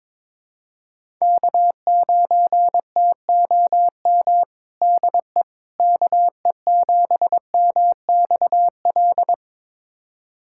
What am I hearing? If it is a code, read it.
Morse code, heard as a keyed tone, about 22 wpm, 700 Hz. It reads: K9TOM DE KE7MXL
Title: USA Callsign DE Callsign 22wpm